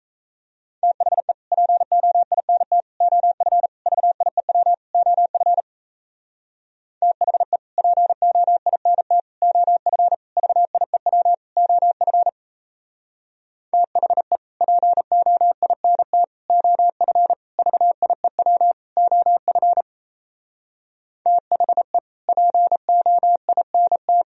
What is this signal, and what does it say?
Morse code, heard as a keyed tone, about 36 wpm, 700 Hz. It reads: THE POINT OF VIEW OF THE POINT OF VIEW OF THE POINT OF VIEW OF THE POINT